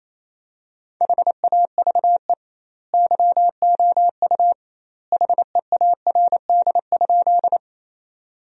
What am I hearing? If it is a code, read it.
Morse code, heard as a keyed tone, about 28 wpm, 700 Hz. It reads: HAVE YOU HEARD?
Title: 2x Word-Spacing - Sentences from Top 400 Words - 28wpm